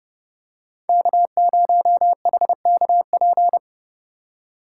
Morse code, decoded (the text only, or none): K0HKP